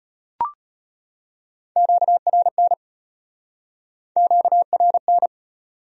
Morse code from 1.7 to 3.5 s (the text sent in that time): QRN